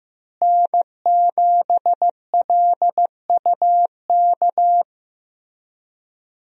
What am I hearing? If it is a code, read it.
Morse code, heard as a keyed tone, about 15 wpm, 700 Hz. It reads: N7LUK
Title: USA Call Signs - 5 Characters - Repeated 3x - 15wpm